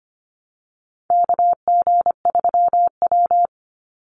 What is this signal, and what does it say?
Morse code, heard as a keyed tone, about 25 wpm, 700 Hz. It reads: KG3W
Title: NA SOTA Chaser Callsigns 25wpm